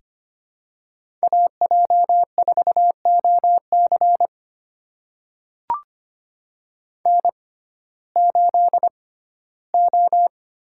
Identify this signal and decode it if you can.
Morse code, heard as a keyed tone, about 25 wpm, spaced as for 15 wpm, 700 Hz. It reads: AJ4OC N8O
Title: USA Call Signs - 5 Characters - 6x Character Spacing - 25wpm